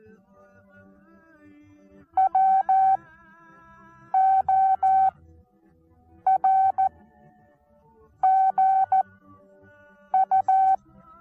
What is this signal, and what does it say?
Morse code, heard as a keyed tone, about 14 wpm, spaced as for 6 wpm, 770 Hz. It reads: WORGU
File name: Onlinelehrgang / morsen / DJ4UF Morsen19